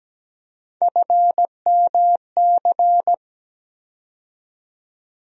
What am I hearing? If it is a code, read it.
Morse code, heard as a keyed tone, about 17 wpm, 700 Hz. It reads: FMC
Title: USA Call Sign Postfixes - 3 Letters 17wpm